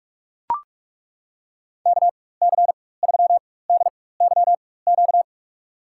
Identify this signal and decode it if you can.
Morse code, heard as a keyed tone, about 41 wpm, spaced as for 20 wpm, 700 Hz. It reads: KC3DYQ
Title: USA Call Signs - 6 Characters - 4x Character Spacing - 45wpm